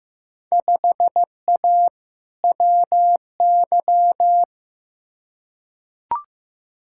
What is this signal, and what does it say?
Morse code, heard as a keyed tone, about 15 wpm, 700 Hz. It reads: 5A WY
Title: Field Day Exchanges Encore - 15wpm